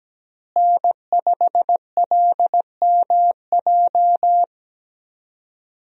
Morse code, heard as a keyed tone, about 17 wpm, 700 Hz. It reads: N5LMJ
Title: USA Call Signs - 5 Characters - Repeated 3x - 17wpm